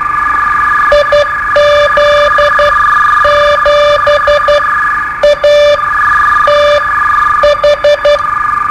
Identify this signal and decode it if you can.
Morse code, heard as a keyed tone, about 12 wpm, spaced as for 9 wpm, 590 Hz, with noise. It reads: IZ7ATH
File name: qso-01